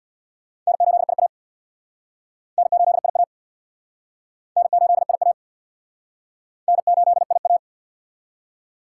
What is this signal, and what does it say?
Morse code, heard as a keyed tone, about 60 wpm, 690 Hz. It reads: N8IA N8IA N8IA N8IA
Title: Speed Racing - USA 1x2 Call Signs 50wpm